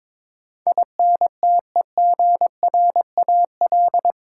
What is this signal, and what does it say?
Morse code, heard as a keyed tone, about 22 wpm, 700 Hz. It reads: INTEGRAL
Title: Sets of 5 Words - New Word Drill - 22wpm